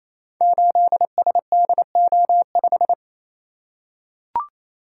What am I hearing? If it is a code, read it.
Morse code, heard as a keyed tone, about 27 wpm, 700 Hz. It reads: 8SDO5
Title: Code Groups - 5 Letter:Numbers - 28wpm